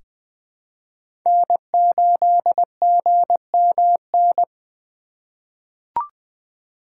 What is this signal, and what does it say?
Morse code, heard as a keyed tone, about 20 wpm, 700 Hz. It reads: N8GMN
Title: USA Call Signs - 5 Characters - 4x Character Spacing - 20wpm